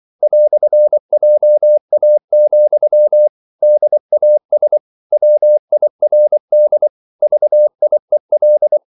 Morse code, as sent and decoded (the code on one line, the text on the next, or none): .-..-. .--- .- --..-- / -.. .- ... / .-- .. .-. -.. / ...- .. .-..
"JA, DAS WIRD VIEL